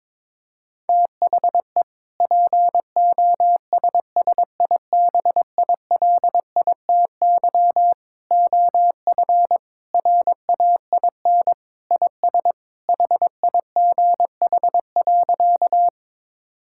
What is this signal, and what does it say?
Morse code, heard as a keyed tone, about 21 wpm, 700 Hz. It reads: THE POSSIBILITY OF RAIN IS HIGH.
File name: practice / 22wpm / Sentences with 3-Word Phrases - 22wpm